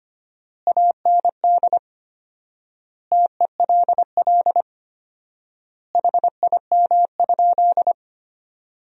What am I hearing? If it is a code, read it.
Morse code, heard as a keyed tone, about 25 wpm, 700 Hz. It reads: AND TELL HIM?
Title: 4x Word-Spacing - Sentences from Top 200 Words - 25wpm